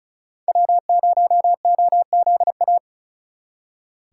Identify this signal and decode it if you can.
Morse code, heard as a keyed tone, about 44 wpm, 700 Hz. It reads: W0OZA